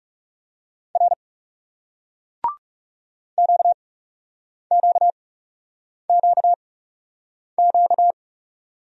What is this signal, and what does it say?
Morse code, heard as a keyed tone, about 39 wpm, spaced as for 8 wpm, 700 Hz. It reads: R QQQQ